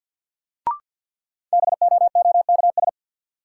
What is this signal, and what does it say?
Morse code, heard as a keyed tone, about 53 wpm, 700 Hz. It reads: BOOKS
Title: Sentences from Top 1200 Words - New Word Drill Alt - 50wpm